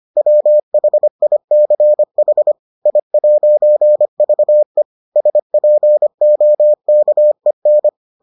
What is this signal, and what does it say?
Morse code, heard as a keyed tone, about 25 wpm, 600 Hz. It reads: WHICH I'VE SPOKEN